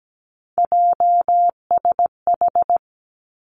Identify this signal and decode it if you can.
Morse code, heard as a keyed tone, about 17 wpm, 700 Hz. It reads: JSH
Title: Three Challenging Letter-Number Combinations 17wpm